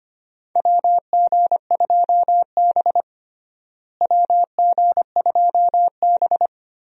Morse code, heard as a keyed tone, about 25 wpm, 700 Hz. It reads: WG2B WG2B
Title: USA Call Signs - 4 Characters - Repeated 2x - 25wpm